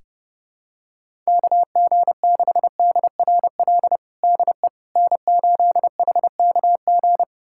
Morse code, heard as a keyed tone, about 32 wpm, 700 Hz. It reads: KG6DRL DE N8HKG
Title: USA Callsign DE Callsign 30wpm